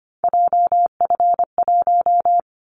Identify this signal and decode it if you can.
Morse code, heard as a keyed tone, about 25 wpm, 700 Hz. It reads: JF1
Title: Callsign Prefixes for Top 25 DXCC - 25wpm